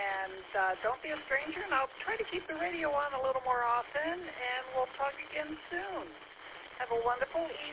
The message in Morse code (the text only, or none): none